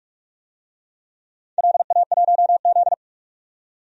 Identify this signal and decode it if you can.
Morse code, heard as a keyed tone, about 47 wpm, 700 Hz. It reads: LA1Z
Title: DXCC-100 Call Signs - 4 Characters - 45wpm